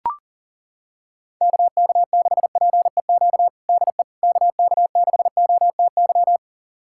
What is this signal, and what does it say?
Morse code, heard as a keyed tone, about 40 wpm, 700 Hz. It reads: KK6PEQ DE KK6OTY
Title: USA Callsign DE Callsign - 6 Characters - 40wpm